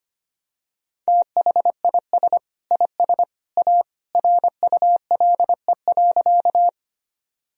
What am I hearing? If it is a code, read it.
Morse code, heard as a keyed tone, about 25 wpm, 700 Hz. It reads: THIS IS A RULE.
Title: Sentences from Top 1000 Words - Review Part 2 - 25wpm